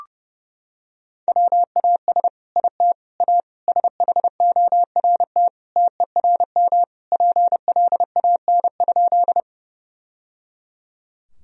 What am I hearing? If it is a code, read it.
Morse code, heard as a keyed tone, about 30 wpm, 700 Hz. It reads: WAS IT A SHORT TERM PLAN?
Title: Sentences from Top 1000 Words - Review Part 4 - 30wpm